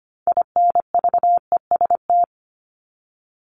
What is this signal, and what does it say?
Morse code, heard as a keyed tone, about 25 wpm, 700 Hz. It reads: INVEST
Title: Top 1001-1250 Words 25wpm